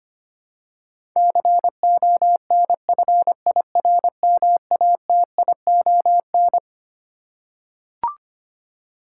Morse code, heard as a keyed tone, about 25 wpm, 700 Hz. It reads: CONFIRMATION
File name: Twelve Letter Words - 25wpm